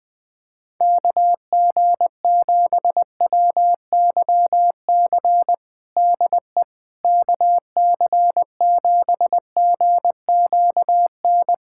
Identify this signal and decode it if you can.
Morse code, heard as a keyed tone, about 20 wpm, 700 Hz. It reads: KG7WYC DE KC7GQN
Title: USA Callsign DE Callsign 20wpm